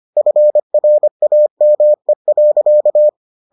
Morse code, heard as a keyed tone, about 25 wpm, 600 Hz. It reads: FRAME.